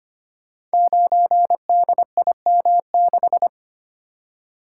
Morse code, heard as a keyed tone, about 25 wpm, 700 Hz. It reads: 9DIM6
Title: Code Groups - 5 Letter:Numbers - 25wpm